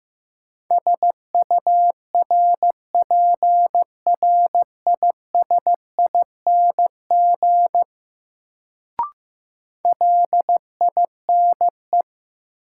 Morse code, heard as a keyed tone, about 15 wpm, 700 Hz. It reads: SURPRISING LINE
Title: Sets of 5 Words - New Word Drill - 15wpm